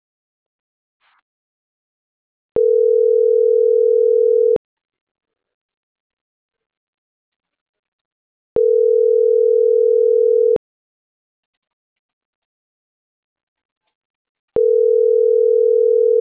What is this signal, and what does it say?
Ringback tone (North America), 440 + 480 Hz.